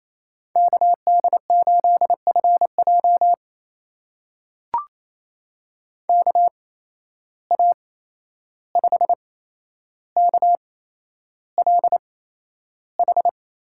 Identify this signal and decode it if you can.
Morse code, heard as a keyed tone, about 28 wpm, spaced as for 11 wpm, 700 Hz. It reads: KD8FJ KA5KLH
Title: USA Call Signs - 8x Character Spacing - 28wpm